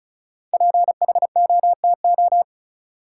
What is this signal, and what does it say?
Morse code, heard as a keyed tone, about 34 wpm, 700 Hz. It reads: PHOTO
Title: Sentences from Top 1200 Words - New Word Drill - 35wpm